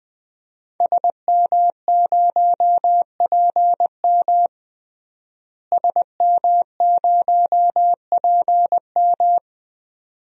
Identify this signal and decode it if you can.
Morse code, heard as a keyed tone, about 20 wpm, 700 Hz. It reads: SM0PM SM0PM
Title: DXCC-100 Call Signs - 5 Characters - Repeated 2x - 20wpm